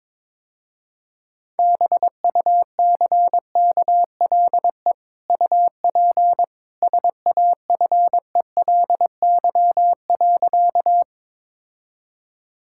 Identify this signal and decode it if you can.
Morse code, heard as a keyed tone, about 22 wpm, 700 Hz. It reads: BUCKLE UP SAFELY.
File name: practice / 22wpm / Easy Sentences with Hints - 22wpm